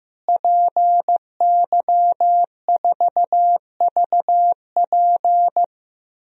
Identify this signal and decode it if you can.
Morse code, heard as a keyed tone, about 15 wpm, 700 Hz. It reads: PY4VP
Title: DXCC-100 Call Signs - 5 Characters - Repeated 3x - 15wpm